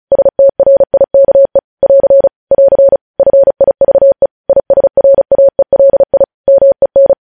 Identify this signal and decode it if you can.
Morse code, heard as a keyed tone, about 36 wpm, 550 Hz. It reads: STRIKE <AR> <AR> FIVE ISRAELI MEN
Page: News Headlines in Morse Code at 35 WPM